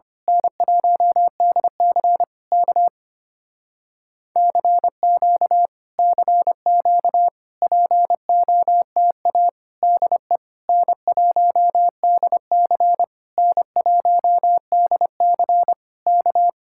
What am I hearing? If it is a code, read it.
Morse code, heard as a keyed tone, about 24 wpm, 700 Hz. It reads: N1DC K CQ CQ POTA DE N1DC N1DC K
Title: Speed-Racing Common QSO Phrases - 25wpm